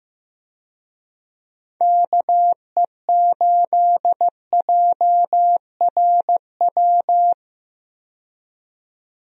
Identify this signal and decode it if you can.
Morse code, heard as a keyed tone, about 15 wpm, 700 Hz. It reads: KE8JRW
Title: USA Call Signs - 6 Characters - 15wpm